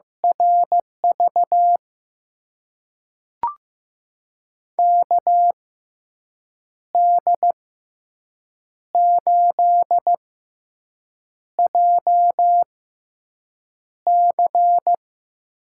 Morse code, heard as a keyed tone, about 15 wpm, spaced as for 6 wpm, 700 Hz. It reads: RV KD8JC
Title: USA Call Signs - 6 Characters - 6x Character Spacing - 15wpm